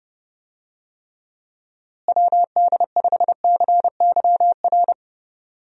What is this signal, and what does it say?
Morse code, heard as a keyed tone, about 31 wpm, 700 Hz. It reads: WD5CYR
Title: USA Call Signs - 6 Characters - Repeated 3x - 30wpm